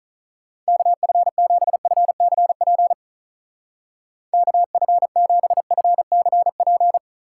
Morse code, heard as a keyed tone, about 45 wpm, 700 Hz. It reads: KF7FCP KF7FCP